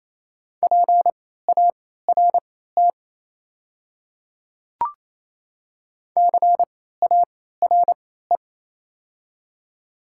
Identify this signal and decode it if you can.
Morse code, heard as a keyed tone, about 27 wpm, spaced as for 16 wpm, 700 Hz. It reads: PART CARE
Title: 3x Character Spacing - Top 500 Words - 28wpm